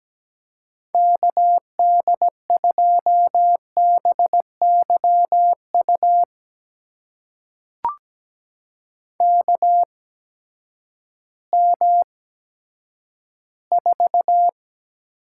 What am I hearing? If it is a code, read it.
Morse code, heard as a keyed tone, about 17 wpm, spaced as for 10 wpm, 700 Hz. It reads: KD2BYU KM4
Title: USA Call Signs - 6 Characters - 8x Character Spacing - 17wpm